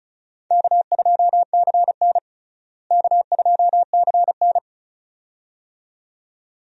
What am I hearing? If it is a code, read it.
Morse code, heard as a keyed tone, about 36 wpm, 700 Hz. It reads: K2CN K2CN
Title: USA Call Signs - 4 Characters - Repeated 2x - 35wpm